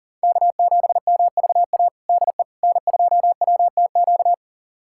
Morse code, heard as a keyed tone, about 40 wpm, 700 Hz. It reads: K7MVA DE N2WTQ
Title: USA Callsign DE Callsign - 5 Characters - 40wpm